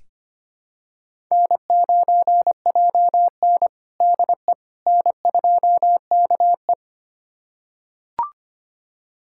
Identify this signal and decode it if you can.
Morse code, heard as a keyed tone, about 25 wpm, 700 Hz. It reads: N9JN DE N2KE
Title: USA Callsign DE Callsign - 4 Characters - 25wpm